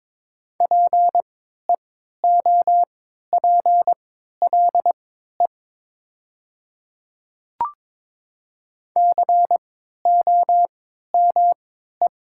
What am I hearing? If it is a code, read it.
Morse code, heard as a keyed tone, about 22 wpm, spaced as for 12 wpm, 700 Hz. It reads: PEOPLE COME
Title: 3x Character Spacing - Top 300 Words - 22wpm